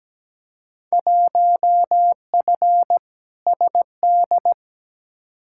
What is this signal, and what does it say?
Morse code, heard as a keyed tone, about 17 wpm, 700 Hz. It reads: 1F SD
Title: Field Day Exchanges - 17wpm